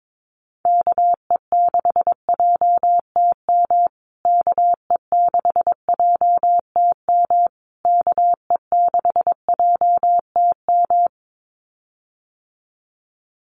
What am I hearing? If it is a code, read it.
Morse code, heard as a keyed tone, about 22 wpm, 700 Hz. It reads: KE6JTM KE6JTM KE6JTM
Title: USA Call Signs Repeated 3x 22wpm